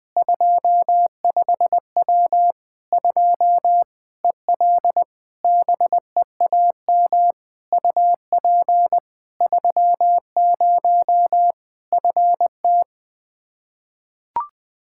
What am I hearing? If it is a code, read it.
Morse code, heard as a keyed tone, about 21 wpm, 700 Hz. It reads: 25W 2 EL BEAM UP 30 FT
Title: Practice QSO - Exchange 2 - Info - 20wpm